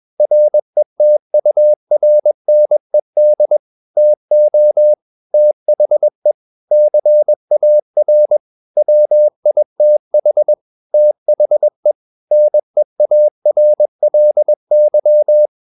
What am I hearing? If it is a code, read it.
Morse code, heard as a keyed tone, about 21 wpm, 600 Hz. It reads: RETURNED TO THE CAR WITH THE NEARLY
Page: Romanen Never af Ken Follet på engelsk.